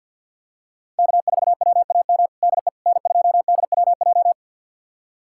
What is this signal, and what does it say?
Morse code, heard as a keyed tone, about 52 wpm, 700 Hz. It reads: K4WAM DE N2DWJ
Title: USA Callsign DE Callsign - 5 Characters - 50wpm